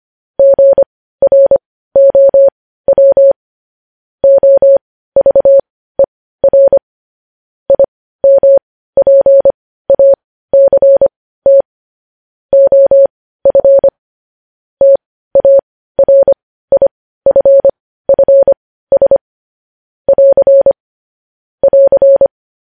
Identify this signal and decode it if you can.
Morse code, heard as a keyed tone, about 25 wpm, spaced as for 15 wpm, 550 Hz. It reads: GROW OVER IMPACT OF TARIFFS <AR> <AR>
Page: news headlines in morse code at 15 wpm